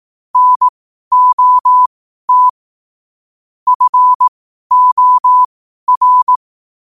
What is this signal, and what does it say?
Morse code, heard as a keyed tone, about 18 wpm, spaced as for 12 wpm, 1000 Hz. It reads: NOT FOR
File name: qotdfiles / 12wpm